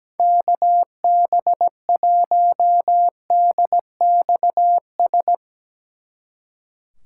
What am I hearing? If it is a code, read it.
Morse code, heard as a keyed tone, about 19 wpm, 700 Hz. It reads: KB1DXS
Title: Speed Racing - USA 2x3 Call Signs 17wpm